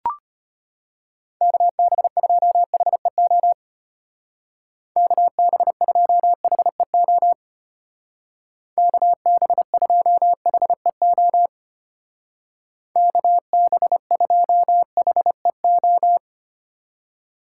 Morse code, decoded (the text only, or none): KB2HEO KB2HEO KB2HEO KB2HEO